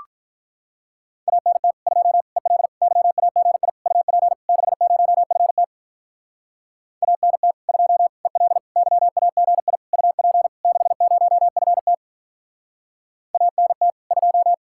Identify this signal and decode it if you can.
Morse code, heard as a keyed tone, about 45 wpm, 700 Hz. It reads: ANT 2 EL YAGI UP 60FT ANT 2 EL YAGI UP 60FT ANT 2